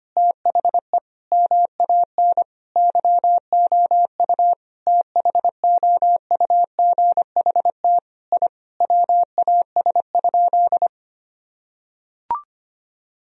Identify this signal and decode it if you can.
Morse code, heard as a keyed tone, about 25 wpm, 700 Hz. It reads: THE MAN YOU THOUGHT I WAS?
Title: Sentences from Top 1000 Words - Review Part 3 - 25wpm